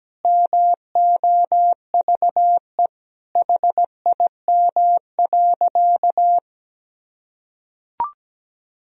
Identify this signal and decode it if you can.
Morse code, heard as a keyed tone, about 17 wpm, 700 Hz. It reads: MOVE HIM.
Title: Sentences from Top 1000 Words - Review Part 1 - 17wpm